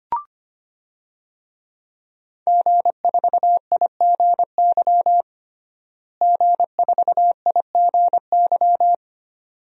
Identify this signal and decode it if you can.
Morse code, heard as a keyed tone, about 25 wpm, 700 Hz. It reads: G4IGY G4IGY
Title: DXCC-100 Call Signs - 5 Characters - Repeated 2x - 25wpm